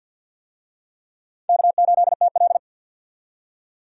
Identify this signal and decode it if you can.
Morse code, heard as a keyed tone, about 46 wpm, 700 Hz. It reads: K8TL
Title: USA Call Signs - 4 Characters - 50wpm